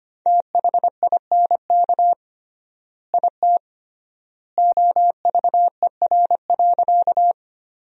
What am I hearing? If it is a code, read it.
Morse code, heard as a keyed tone, about 25 wpm, 700 Hz. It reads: THINK IT OVER.
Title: 3x Word-Spacing - Sentences from Top 200 Words - 25wpm